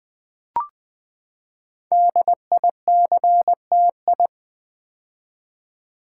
Morse code, heard as a keyed tone, about 20 wpm, 700 Hz. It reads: DICTI
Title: Build A Word - 5 Letters At A Time - 10-Letter Words - 20wpm